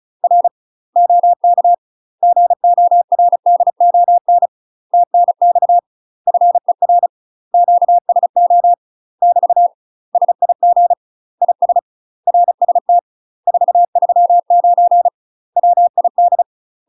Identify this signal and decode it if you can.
Morse code, heard as a keyed tone, about 36 wpm, 700 Hz. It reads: R OK GORDON TNX FER QSO <BT> SIG IS RST 439 WID
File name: UK MASTER QSOs TRAINING / 35WPM / UK MASTER QSOs 0007